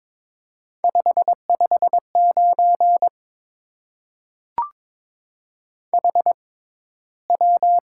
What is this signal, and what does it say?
Morse code, heard as a keyed tone, about 22 wpm, spaced as for 13 wpm, 700 Hz. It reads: 559 HW
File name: Top 90 QSO Elements - 6x Character Spacing - 22wpm